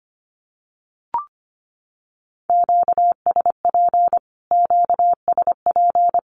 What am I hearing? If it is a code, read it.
Morse code, heard as a keyed tone, about 25 wpm, 700 Hz. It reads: QSP QSP
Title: Q-Codes Repeated 3x 25wpm